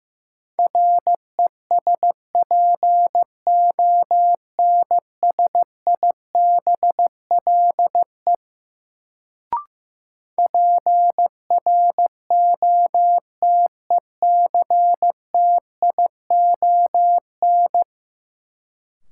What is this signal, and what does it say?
Morse code, heard as a keyed tone, about 15 wpm, 700 Hz. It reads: RESPONSIBLE PROTECTION
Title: Sentences from Top 1600 Words - New Word Drill Alt - 15wpm